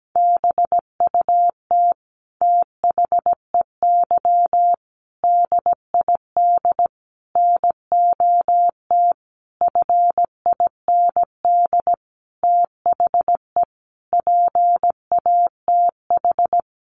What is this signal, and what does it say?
Morse code, heard as a keyed tone, about 17 wpm, 700 Hz. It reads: BUT THEY DID NOT FIND THE PATH
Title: Sentences from Top 1400 Words 17wpm